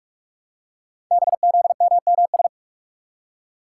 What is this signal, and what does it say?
Morse code, heard as a keyed tone, about 44 wpm, 700 Hz. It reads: BZMMS
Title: Code Groups - 5 Letters - 45wpm